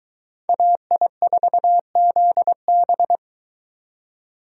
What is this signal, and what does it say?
Morse code, heard as a keyed tone, about 22 wpm, 700 Hz. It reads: AI4ZB